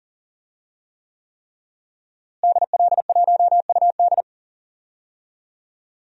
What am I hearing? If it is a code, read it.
Morse code, heard as a keyed tone, about 40 wpm, 700 Hz. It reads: DL1UD
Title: DXCC-100 Call Signs - 5 Characters - Repeated 3x - 40wpm